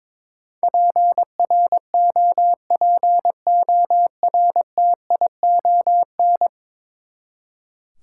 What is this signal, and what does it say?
Morse code, heard as a keyed tone, about 22 wpm, 700 Hz. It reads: PROPORTION
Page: morse-code-ninja-practice-22wpm